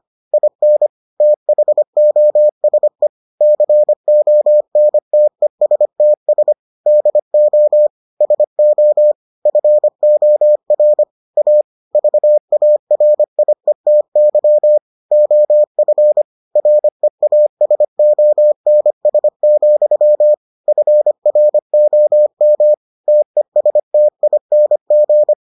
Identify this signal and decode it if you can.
Morse code, heard as a keyed tone, about 25 wpm, 600 Hz. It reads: IN THOSE CONTESTS DO SO FOR A VARIETY OF REASONS, FROM TESTING